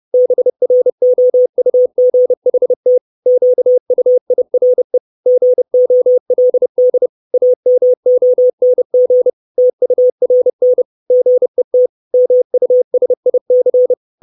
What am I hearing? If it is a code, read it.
Morse code, heard as a keyed tone, about 30 wpm, 500 Hz. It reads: BROUGHT QUIRE GOLD AMONG TURN GET MUSIC
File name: 632 Words 30WPM